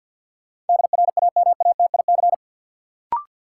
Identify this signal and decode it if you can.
Morse code, heard as a keyed tone, about 49 wpm, 700 Hz. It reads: DRAMATIC
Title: Sentences from Top 1300 Words - New Word Drill - 50wpm